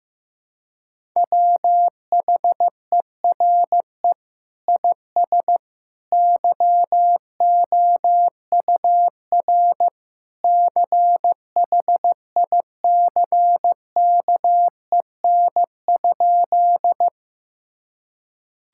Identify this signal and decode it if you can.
Morse code, heard as a keyed tone, about 16 wpm, 700 Hz. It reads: WHERE IS YOUR CHICKEN?
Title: Sentences from Top 1000 Words - Review Part 5 - 15wpm